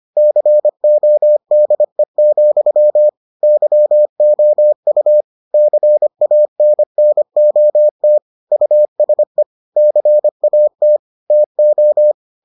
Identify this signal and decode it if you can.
Morse code, heard as a keyed tone, about 25 wpm, 600 Hz. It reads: CODE, YOU CANNOT USE CAT TO